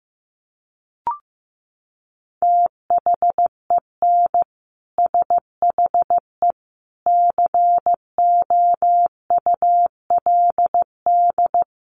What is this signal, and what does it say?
Morse code, heard as a keyed tone, about 15 wpm, 700 Hz. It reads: THEN SHE COULD
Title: Sentences from Top 1400 Words 15wpm